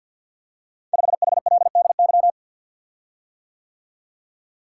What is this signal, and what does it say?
Morse code, heard as a keyed tone, about 52 wpm, 690 Hz. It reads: 5HLDY
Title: Code Groups - 5 Letter:Numbers - 50wpm